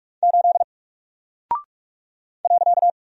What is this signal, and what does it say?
Morse code, heard as a keyed tone, about 46 wpm, 700 Hz. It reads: 8 .
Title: Single Character (Letter:Number:Punctuation:Prosign) 45wpm